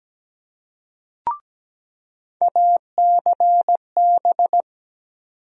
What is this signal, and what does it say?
Morse code, heard as a keyed tone, about 17 wpm, 700 Hz. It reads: ACB
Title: USA Call Sign Postfixes - 3 Letters 17wpm